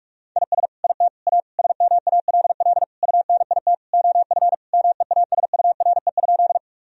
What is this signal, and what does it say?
Morse code, heard as a keyed tone, about 46 wpm, 700 Hz. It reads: IS IT A SMALL UNIT OF MEASURE?
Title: Sentences from Top 1000 Words - Review Part 2 - 45wpm